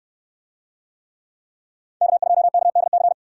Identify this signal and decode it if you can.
Morse code, heard as a keyed tone, about 68 wpm, 700 Hz. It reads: K2KDC